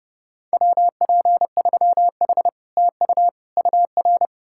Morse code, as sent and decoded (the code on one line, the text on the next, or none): .-- .--. ...-- .... / - ..- / ..- .-.
WP3H TU UR